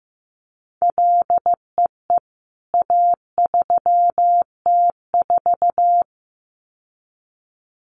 Morse code, heard as a keyed tone, about 15 wpm, 700 Hz. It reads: LEE A3T4
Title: CWT Exchange 15wpm